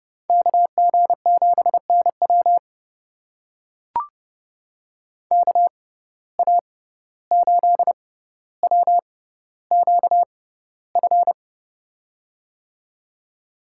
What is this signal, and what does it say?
Morse code, heard as a keyed tone, about 30 wpm, spaced as for 15 wpm, 700 Hz. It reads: KG7NW KA8WQF
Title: USA Call Signs - 6x Character Spacing - 30wpm